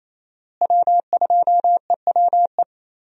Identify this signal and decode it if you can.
Morse code, heard as a keyed tone, about 28 wpm, 700 Hz. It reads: W2EWE
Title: USA Call Signs - 5 Characters - Repeated 3x - 28wpm